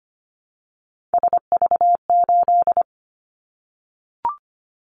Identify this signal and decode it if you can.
Morse code, heard as a keyed tone, about 25 wpm, 700 Hz. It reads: SV8
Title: Callsign Prefixes for Top 50 DXCC - 25wpm